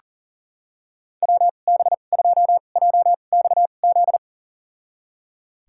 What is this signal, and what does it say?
Morse code, heard as a keyed tone, about 40 wpm, spaced as for 29 wpm, 700 Hz. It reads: WB2JXZ
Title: USA Call Signs - 6 Characters - 2x Character Spacing - 40wpm